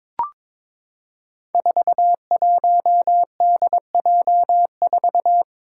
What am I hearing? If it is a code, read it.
Morse code, heard as a keyed tone, about 22 wpm, 700 Hz. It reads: 41DJ4
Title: Five Challenging Letter-Number Combinations 22wpm